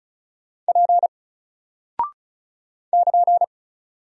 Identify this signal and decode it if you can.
Morse code, heard as a keyed tone, about 36 wpm, 700 Hz. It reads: P <KN>
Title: Single Character (Letter:Number:Punctuation:Prosign) Alt 35wpm